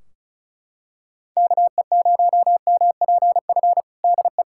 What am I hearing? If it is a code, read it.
Morse code, heard as a keyed tone, about 34 wpm, 700 Hz. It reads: KE0MPF DE
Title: Practice QSO - Exchange 4 - Ending - 35wpm